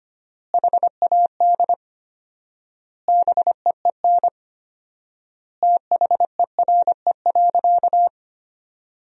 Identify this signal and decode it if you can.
Morse code, heard as a keyed tone, about 25 wpm, 700 Hz. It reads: HAD BEEN THERE.
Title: 4x Word-Spacing - Sentences from Top 300 Words - 25wpm